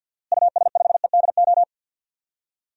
Morse code, heard as a keyed tone, about 46 wpm, 700 Hz. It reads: US5EDO
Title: DXCC-100 Call Signs - 6 Characters - Repeated 2x - 50wpm